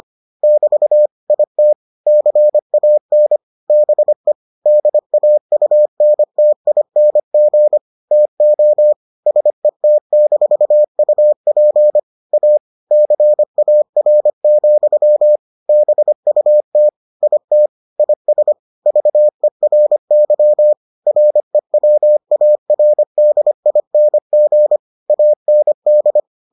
Morse code, heard as a keyed tone, about 25 wpm, 600 Hz. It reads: <BT> IT CAN BE DAUNTING TO SET-UP A CAR, BUT IT IS VERY REWARDING AND